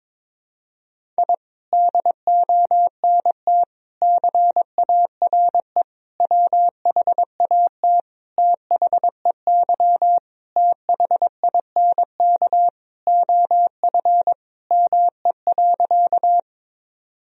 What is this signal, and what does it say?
Morse code, heard as a keyed tone, about 22 wpm, 700 Hz. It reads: I DONT CARE WHAT THEY THINK OF ME.
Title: Sentences from Top 1000 Words - Review Part 1 - 22wpm